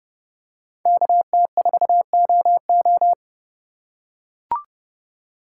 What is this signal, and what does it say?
Morse code, heard as a keyed tone, about 30 wpm, 700 Hz. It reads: KT4OO
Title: USA Call Signs - 3x Character Spacing - 30wpm